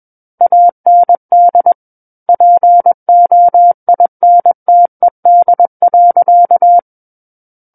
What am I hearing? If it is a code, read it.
Morse code, heard as a keyed tone, about 21 wpm, 700 Hz. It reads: AND POINTED.
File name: Book One / 12-Chapter 12 - 21 WPM